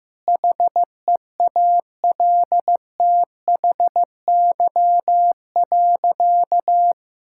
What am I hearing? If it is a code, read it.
Morse code, heard as a keyed tone, about 15 wpm, 700 Hz. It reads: HEALTHY.
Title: Sentences from Top 1000 Words - Review Part 5 - 15wpm